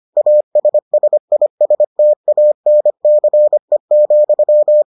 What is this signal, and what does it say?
Morse code, heard as a keyed tone, about 25 wpm, 600 Hz. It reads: ASSISTANCE,